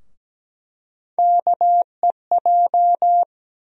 Morse code, heard as a keyed tone, about 17 wpm, 700 Hz. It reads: KEJ
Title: USA Call Sign Postfixes - 3 Letters 17wpm